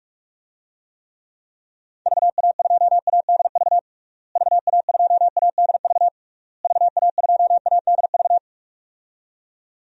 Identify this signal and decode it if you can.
Morse code, heard as a keyed tone, about 41 wpm, 700 Hz. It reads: VA2ADV VA2ADV VA2ADV
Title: DXCC-100 Call Signs - 6 Characters - Repeated 3x - 45wpm